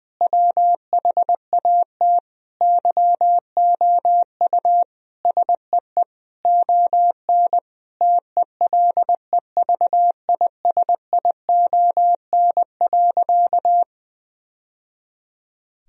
Morse code, heard as a keyed tone, about 20 wpm, 700 Hz. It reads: WHAT YOU SEE ON TELEVISION.
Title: Sentences from Top 1000 Words - Review Part 4 - 20wpm